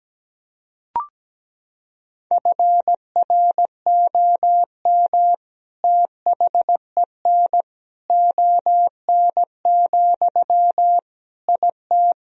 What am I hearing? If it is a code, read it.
Morse code, heard as a keyed tone, about 17 wpm, 700 Hz. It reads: FROM THEN ON, IT
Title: Sentences from Top 1000 Words - Review Part 2 - 17wpm